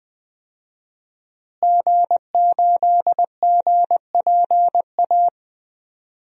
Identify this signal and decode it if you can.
Morse code, heard as a keyed tone, about 20 wpm, 700 Hz. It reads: G8GPA
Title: DXCC-100 Call Signs - 5 Characters - Repeated 2x - 20wpm